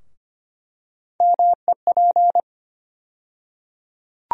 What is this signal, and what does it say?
Morse code, heard as a keyed tone, about 25 wpm, 700 Hz. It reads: MEP
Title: USA Call Sign Postfixes - 3 Letters 25wpm